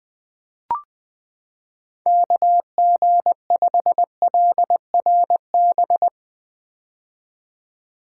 Morse code, heard as a keyed tone, about 21 wpm, 700 Hz. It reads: KG5LRB
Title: 2023 POTA Activator Call Signs - 20wpm